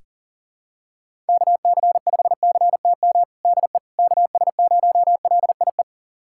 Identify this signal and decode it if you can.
Morse code, heard as a keyed tone, about 40 wpm, 700 Hz. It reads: KC5CTM DE KS0LIE
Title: USA Callsign DE Callsign - 6 Characters - 40wpm